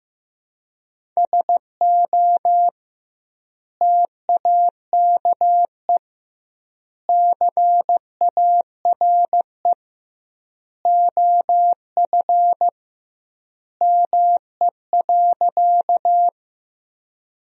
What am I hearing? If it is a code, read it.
Morse code, heard as a keyed tone, about 15 wpm, 700 Hz. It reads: SO TAKE CARE OF ME.
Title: 2x Word-Spacing - Sentences from Top 300 Words - 15wpm